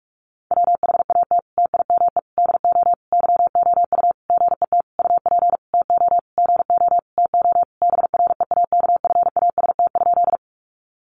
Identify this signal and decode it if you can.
Morse code, heard as a keyed tone, about 44 wpm, 700 Hz. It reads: WHAT TIME DO YOU GET UP TO GO TO BREAKFAST?
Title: Sentences from Top 1500 Words 45wpm